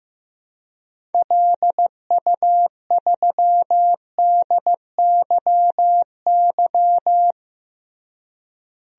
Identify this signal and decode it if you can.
Morse code, heard as a keyed tone, about 15 wpm, 700 Hz. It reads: LU3DYY
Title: DXCC-100 Call Signs - 6 Characters - Repeated 2x - 15wpm